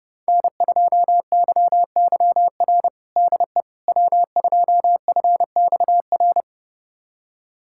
Morse code, heard as a keyed tone, about 30 wpm, 700 Hz. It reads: N2YYR DE W2FXR
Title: USA Callsign DE Callsign - 5 Characters - 30wpm